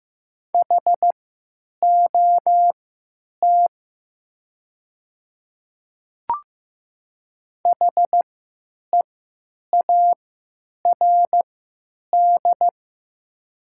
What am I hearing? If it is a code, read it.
Morse code, heard as a keyed tone, about 15 wpm, spaced as for 9 wpm, 700 Hz. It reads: HOT HEARD
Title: 3x Character Spacing - Top 500 Words - 15wpm